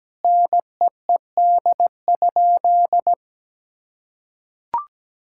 Morse code, heard as a keyed tone, about 17 wpm, 700 Hz. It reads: NEED?
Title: Sentences from Top 1000 Words - Review Part 5 - 17wpm